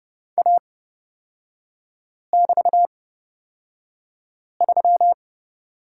Morse code, heard as a keyed tone, about 30 wpm, spaced as for 5 wpm, 700 Hz. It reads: A<BT>3
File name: Lesson 36e - BT - Three Characters 30wpm